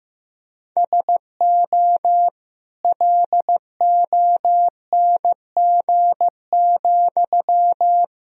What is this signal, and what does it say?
Morse code, heard as a keyed tone, about 15 wpm, 700 Hz. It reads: SO LONG,
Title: Sentences from Top 1000 Words - Review Part 4 - 15wpm